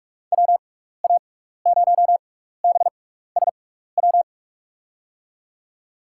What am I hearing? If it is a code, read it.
Morse code, heard as a keyed tone, about 46 wpm, spaced as for 16 wpm, 700 Hz. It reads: WA0BSW
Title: USA Call Signs - 6 Characters - 6x Character Spacing - 45wpm